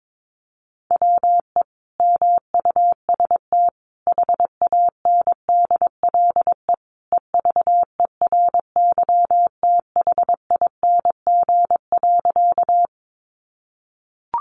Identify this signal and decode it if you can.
Morse code, heard as a keyed tone, about 22 wpm, 700 Hz. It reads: WE MUST HANDLE EVERYTHING.
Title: Sentences from Top 1100 Words 22wpm